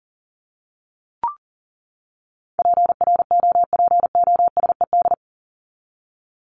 Morse code, heard as a keyed tone, about 40 wpm, 700 Hz. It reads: PROPOSED
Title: ED Suffix - Top 100 Words with Base Words 40wpm